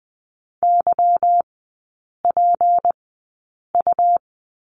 Morse code, heard as a keyed tone, about 20 wpm, spaced as for 8 wpm, 700 Hz. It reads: YPU
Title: 166 Lesson 27e - Number 7 - Three Characters 20wpm